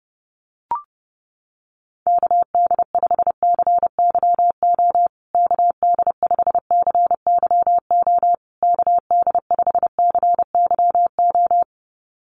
Morse code, heard as a keyed tone, about 28 wpm, 700 Hz. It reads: KD5CYO KD5CYO KD5CYO
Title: USA Call Signs Repeated 3x 30wpm